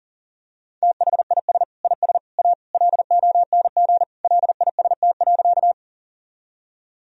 Morse code, heard as a keyed tone, about 40 wpm, 700 Hz. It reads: THIS IS A LONG LIST.
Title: Sentences from Top 1000 Words - Review Part 2 - 40wpm